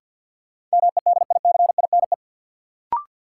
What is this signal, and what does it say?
Morse code, heard as a keyed tone, about 53 wpm, 700 Hz. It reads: MEDICINE
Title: Sentences from Top 900 Words - New Word Drill - 50wpm